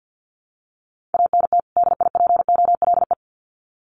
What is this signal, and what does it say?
Morse code, heard as a keyed tone, about 50 wpm, 700 Hz. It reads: ANT DIPOLE
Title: Common QSO Phrases 50wpm